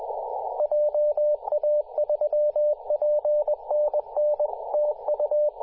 Morse code, heard as a keyed tone, about 20 wpm, 600 Hz, with radio noise. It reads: JA3PNN TU
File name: posts / 2017-sep-21-0819zt88xa-10-116mhz-cw